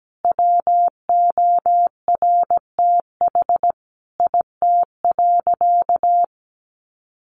Morse code, heard as a keyed tone, about 17 wpm, 700 Hz. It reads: WORTH IT.
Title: Sentences from Top 1300 Words 17wpm